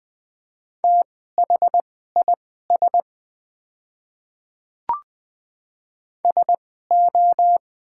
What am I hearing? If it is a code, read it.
Morse code, heard as a keyed tone, about 20 wpm, spaced as for 14 wpm, 700 Hz. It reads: THIS SO
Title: 2x Character Spacing - Top 200 Words - 20wpm